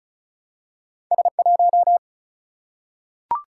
Morse code, heard as a keyed tone, about 36 wpm, 700 Hz. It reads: S 1
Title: Single-Letter Single-Number 35wpm